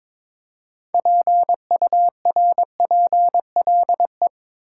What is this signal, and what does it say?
Morse code, heard as a keyed tone, about 22 wpm, 700 Hz. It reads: PURPLE
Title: Six Letter Words - 22wpm